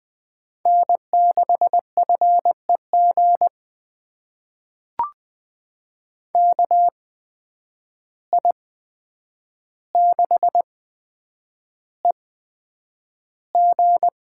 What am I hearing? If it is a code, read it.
Morse code, heard as a keyed tone, about 20 wpm, spaced as for 9 wpm, 700 Hz. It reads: N6FEG KI6EG
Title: USA Call Signs - 5 Characters - 8x Character Spacing - 20wpm